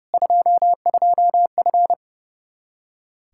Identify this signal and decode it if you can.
Morse code, heard as a keyed tone, about 32 wpm, 700 Hz. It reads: 22F
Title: Practice QSO - Exchange 2 - Info - 30wpm